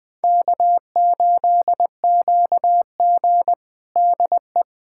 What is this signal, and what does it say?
Morse code, heard as a keyed tone, about 20 wpm, 700 Hz. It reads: K8QG DE
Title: USA Callsign DE Callsign - 4 Characters - 20wpm